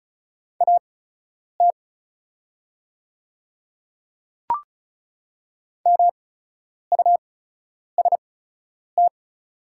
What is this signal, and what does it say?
Morse code, heard as a keyed tone, about 36 wpm, spaced as for 10 wpm, 700 Hz. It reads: AT MUST
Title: 8x Character Spacing - Top 300 Words - 35wpm